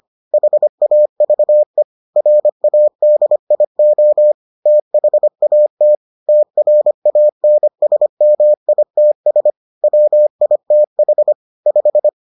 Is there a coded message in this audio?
Yes: Morse code from 0.3 to 12.1 s, heard as a keyed tone, about 25 wpm, 600 Hz. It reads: HAVE RADIO THAT TRANSMITS WITH 5